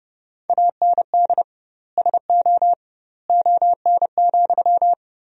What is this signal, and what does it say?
Morse code, heard as a keyed tone, about 30 wpm, 700 Hz. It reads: AND SO ON,
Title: 2x Word-Spacing - Sentences from Top 100 Words - 30wpm